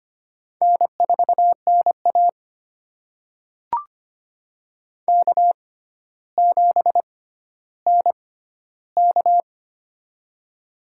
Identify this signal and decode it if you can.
Morse code, heard as a keyed tone, about 25 wpm, spaced as for 13 wpm, 700 Hz. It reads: N4NA K7NK
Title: USA Call Signs - 4 Characters - 6x Character Spacing - 25wpm